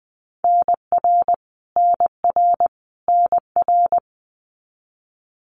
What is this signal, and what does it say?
Morse code, heard as a keyed tone, about 20 wpm, 700 Hz. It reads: NR NR NR
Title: Abbreviations Repeated 3x - 20wpm